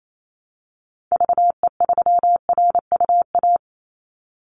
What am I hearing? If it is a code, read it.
Morse code, heard as a keyed tone, about 28 wpm, 700 Hz. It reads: VE3RUA
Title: Call Signs - 28wpm